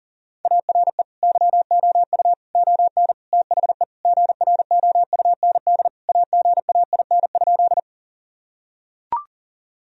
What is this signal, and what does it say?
Morse code, heard as a keyed tone, about 40 wpm, 700 Hz. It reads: ARE YOU ON THE GROUND AGAIN?
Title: Sentences from Top 1000 Words - Review Part 2 - 40wpm